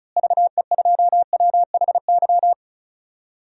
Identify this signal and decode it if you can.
Morse code, heard as a keyed tone, about 36 wpm, 700 Hz. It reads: VE2WHY
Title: DXCC-100 Call Signs - 6 Characters - Repeated 3x - 35wpm